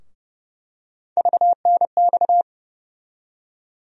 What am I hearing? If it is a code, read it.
Morse code, heard as a keyed tone, about 29 wpm, 700 Hz. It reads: VNX
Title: USA Call Sign Postfixes - 3 Letters 30wpm